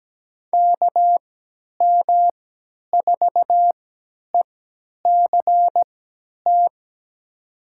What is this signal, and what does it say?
Morse code, heard as a keyed tone, about 17 wpm, spaced as for 10 wpm, 700 Hz. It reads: KM4ECT
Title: USA Call Signs - 6 Characters - 3x Character Spacing - 17wpm